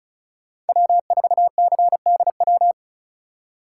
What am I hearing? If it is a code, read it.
Morse code, heard as a keyed tone, about 36 wpm, 700 Hz. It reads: W4CDW
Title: USA Call Signs - 5 Characters - 2x Character Spacing - 35wpm